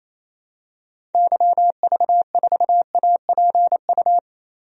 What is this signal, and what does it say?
Morse code, heard as a keyed tone, about 28 wpm, 700 Hz. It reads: YV4APU
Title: DXCC-100 Call Signs - 6 Characters - 28wpm